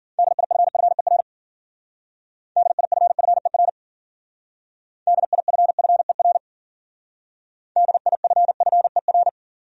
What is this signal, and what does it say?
Morse code, heard as a keyed tone, about 50 wpm, 700 Hz. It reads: DIFFER DIFFER DIFFER DIFFER
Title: Speed-Racing Top 200 Words 40wpm